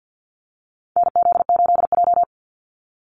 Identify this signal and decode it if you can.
Morse code, heard as a keyed tone, about 50 wpm, 700 Hz. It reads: NZ8J
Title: Call Signs - 50wpm